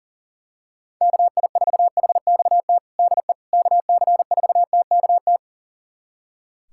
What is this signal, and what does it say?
Morse code, heard as a keyed tone, about 40 wpm, 700 Hz. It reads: KI4HXT DE KC4TKT
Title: USA Callsign DE Callsign - 6 Characters - 40wpm